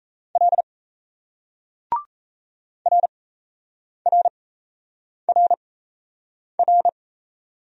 Morse code, heard as a keyed tone, about 37 wpm, spaced as for 8 wpm, 700 Hz. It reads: L RRRR